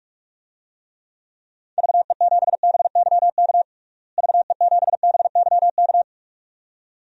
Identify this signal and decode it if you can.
Morse code, heard as a keyed tone, about 46 wpm, 700 Hz. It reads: VE7BYK VE7BYK
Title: DXCC-100 Call Signs - 6 Characters - Repeated 2x - 45wpm